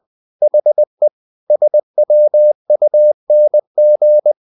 Morse code, heard as a keyed tone, about 20 wpm, 600 Hz. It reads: HE SWUNG
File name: the old man and the sea part 2 20wpm